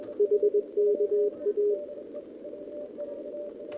Morse code, heard as a keyed tone, about 21 wpm, 410 Hz. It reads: HKA
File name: OK QRP 2026 / OKQRP2026